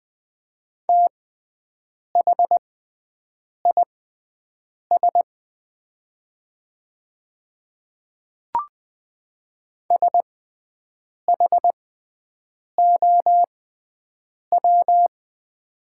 Morse code, heard as a keyed tone, about 20 wpm, spaced as for 7 wpm, 700 Hz. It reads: THIS SHOW
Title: 6x Character Spacing - Top 200 Words - 20wpm